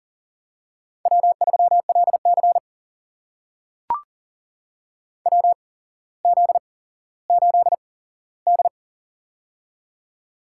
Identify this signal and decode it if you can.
Morse code, heard as a keyed tone, about 40 wpm, spaced as for 17 wpm, 700 Hz. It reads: W3LC WZ8D
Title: USA Call Signs - 4 Characters - 8x Character Spacing - 40wpm